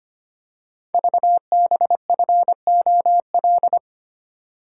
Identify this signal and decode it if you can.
Morse code, heard as a keyed tone, about 25 wpm, 700 Hz. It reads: VBFOL
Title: Code Groups - 5 Letters - 25wpm